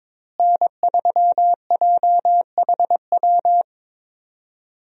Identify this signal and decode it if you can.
Morse code, heard as a keyed tone, about 22 wpm, 700 Hz. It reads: N3JHW
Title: USA Call Signs - 5 Characters - Repeated 2x - 22wpm